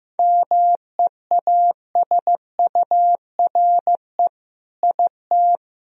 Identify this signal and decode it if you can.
Morse code, heard as a keyed tone, about 15 wpm, 700 Hz. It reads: MEASURE IT
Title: Sentences from Top 1000 Words - Review Part 2 - 15wpm